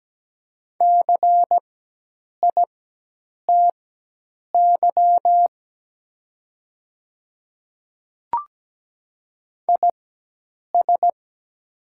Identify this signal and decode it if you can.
Morse code, heard as a keyed tone, about 17 wpm, spaced as for 8 wpm, 700 Hz. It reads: CITY IS
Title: 4x Character Spacing - Top 300 Words - 17wpm